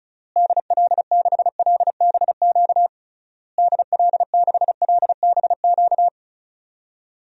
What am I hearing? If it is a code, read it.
Morse code, heard as a keyed tone, about 36 wpm, 700 Hz. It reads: DL6LBQ DL6LBQ
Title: DXCC-100 Call Signs - 6 Characters - Repeated 3x - 35wpm